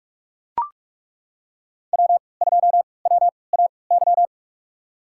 Morse code, heard as a keyed tone, about 42 wpm, spaced as for 25 wpm, 700 Hz. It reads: W2WAY
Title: USA Call Signs - 3x Character Spacing - 45wpm